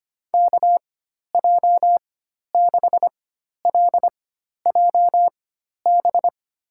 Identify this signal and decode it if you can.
Morse code, heard as a keyed tone, about 25 wpm, spaced as for 12 wpm, 700 Hz. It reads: KJ6LJB
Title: USA Call Signs - 6 Characters - 4x Character Spacing - 25wpm